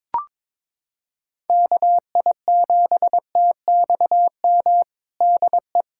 Morse code, heard as a keyed tone, about 22 wpm, 700 Hz. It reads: KI7TXM DE
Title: USA Callsign DE Callsign - 6 Characters - 22wpm